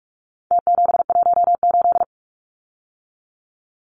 Morse code, heard as a keyed tone, about 44 wpm, 700 Hz. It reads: T718
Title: Numbers - Contest Style - 4-Digit 45wpm